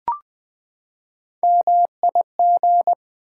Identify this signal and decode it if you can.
Morse code, heard as a keyed tone, about 20 wpm, 700 Hz. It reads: MIG
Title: Build A Word - 3 Letters At A Time - 9-Letter Words - 20wpm